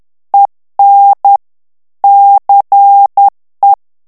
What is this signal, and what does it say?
Morse code, heard as a keyed tone, about 10 wpm, 800 Hz. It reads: EN CE
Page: Français - texte en clair 1 - vitesse : 10 mots minute